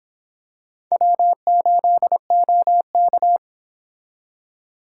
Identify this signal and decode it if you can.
Morse code, heard as a keyed tone, about 26 wpm, 700 Hz. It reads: W8OK